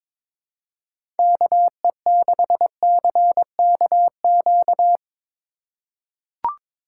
Morse code, heard as a keyed tone, about 22 wpm, 700 Hz. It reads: KE6CKQ
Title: USA Call Signs - 6 Characters - 2x Character Spacing - 22wpm